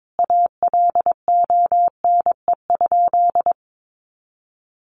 Morse code, heard as a keyed tone, about 22 wpm, 700 Hz. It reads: ALONE?
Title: Sentences from Top 1000 Words 22wpm